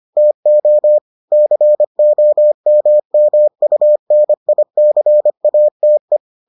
Morse code, heard as a keyed tone, about 25 wpm, 600 Hz. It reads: TO COMMUNICATE